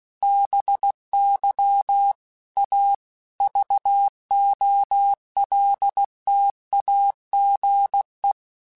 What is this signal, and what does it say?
Morse code, heard as a keyed tone, about 16 wpm, 800 Hz. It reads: BY A VOLTAGE
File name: Technical Subjects / Technical Subjects-mp3s / XL @15